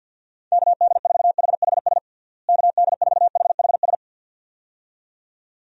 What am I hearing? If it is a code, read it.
Morse code, heard as a keyed tone, about 53 wpm, 700 Hz. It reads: KD4HHS KD4HHS
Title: USA Call Signs - 6 Characters - Repeated 2x - 50wpm